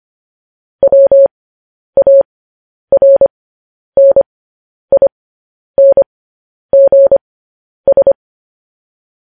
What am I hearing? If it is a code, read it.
Morse code, heard as a keyed tone, about 25 wpm, spaced as for 10 wpm, 550 Hz. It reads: WARNINGS
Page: News Headlines in Morse Code at 10 WPM